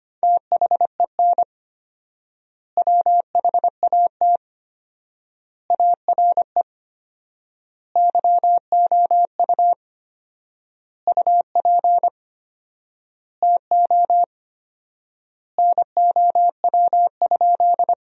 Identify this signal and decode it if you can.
Morse code, heard as a keyed tone, about 25 wpm, 700 Hz. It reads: THEN WHAT ARE YOU UP TO NOW?
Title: 4x Word-Spacing - Sentences from Top 100 Words - 25wpm